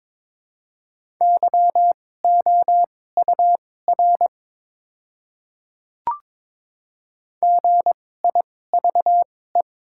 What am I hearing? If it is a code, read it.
Morse code, heard as a keyed tone, about 22 wpm, spaced as for 16 wpm, 700 Hz. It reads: YOUR GIVE
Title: 2x Character Spacing - Top 500 Words - 22wpm